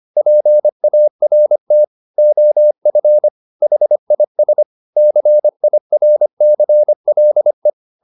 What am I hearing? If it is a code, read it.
Morse code, heard as a keyed tone, about 25 wpm, 600 Hz. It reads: PART OF HIS CIRCLE